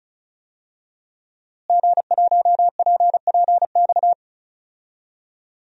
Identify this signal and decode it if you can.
Morse code, heard as a keyed tone, about 36 wpm, 700 Hz. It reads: G1PPX
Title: DXCC-100 Call Signs - 5 Characters - Repeated 2x - 35wpm